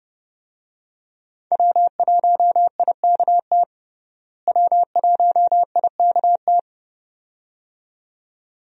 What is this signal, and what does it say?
Morse code, heard as a keyed tone, about 30 wpm, 700 Hz. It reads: W1IKT W1IKT
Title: USA Call Signs - 5 Characters - Repeated 2x - 30wpm